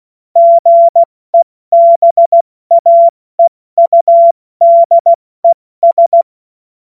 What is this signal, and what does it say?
Morse code, heard as a keyed tone, about 16 wpm, 680 Hz. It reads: GEBAEUDES
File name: Morsekurs / Witze / Witz 07.November 2023 70 F80